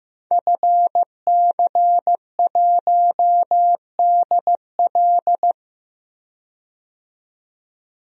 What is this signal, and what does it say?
Morse code, heard as a keyed tone, about 15 wpm, 700 Hz. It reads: FC1DL
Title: DXCC-100 Call Signs - 5 Characters - 15wpm